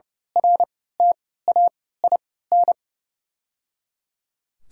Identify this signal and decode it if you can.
Morse code, heard as a keyed tone, about 30 wpm, spaced as for 17 wpm, 700 Hz. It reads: RTAIN
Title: 3x Character Spacing - Top 500 Words - 30wpm